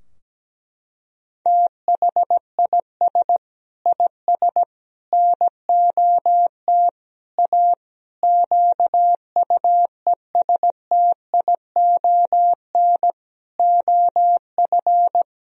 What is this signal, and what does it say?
Morse code, heard as a keyed tone, about 17 wpm, 700 Hz. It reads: THIS IS NOT A QUESTION OF
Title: Sentences from Top 1000 Words - Review Part 2 - 17wpm